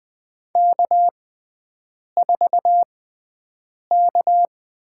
Morse code, heard as a keyed tone, about 20 wpm, spaced as for 7 wpm, 700 Hz. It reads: K4K
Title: USA Call Signs - 5 Characters - 6x Character Spacing - 20wpm